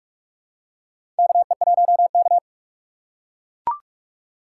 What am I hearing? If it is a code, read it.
Morse code, heard as a keyed tone, about 42 wpm, 700 Hz. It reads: KE1K
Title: USA Call Signs - 4 Characters - 3x Character Spacing - 45wpm